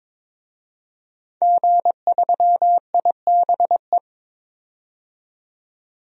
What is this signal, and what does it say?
Morse code, heard as a keyed tone, about 22 wpm, 700 Hz. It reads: G3IBE
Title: DXCC-100 Call Signs - 5 Characters - 22wpm